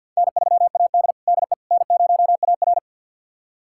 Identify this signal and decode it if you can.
Morse code, heard as a keyed tone, about 48 wpm, 700 Hz. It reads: N3AD DE N0AR
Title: USA Callsign DE Callsign - 4 Characters - 50wpm